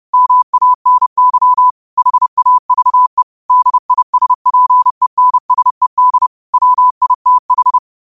Morse code, heard as a keyed tone, about 30 wpm, 1000 Hz. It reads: MANY HAVE DISPENSED WITH
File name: qotdfiles / 30wpm